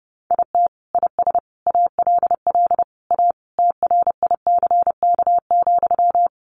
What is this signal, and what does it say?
Morse code, heard as a keyed tone, about 30 wpm, 700 Hz. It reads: IT IS ALL A TRICK,
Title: Sentences from Top 1500 Words 30wpm